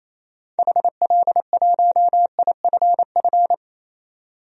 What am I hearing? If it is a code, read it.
Morse code, heard as a keyed tone, about 28 wpm, 700 Hz. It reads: HL1IFF
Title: DXCC-100 Call Signs - 6 Characters - 28wpm